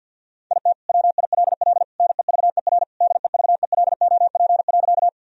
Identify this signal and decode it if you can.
Morse code, heard as a keyed tone, about 52 wpm, 700 Hz. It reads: IT WILL NEVER DEVELOP.
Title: Sentences from Top 1000 Words - Review Part 2 - 50wpm